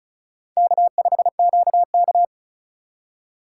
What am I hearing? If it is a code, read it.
Morse code, heard as a keyed tone, about 34 wpm, 700 Hz. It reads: K5QK
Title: USA Call Signs - 4 Characters - 3x Character Spacing - 35wpm